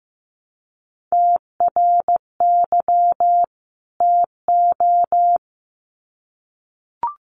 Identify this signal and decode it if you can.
Morse code, heard as a keyed tone, about 15 wpm, 700 Hz. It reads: TRY TO
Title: Sets of 2 Words 15wpm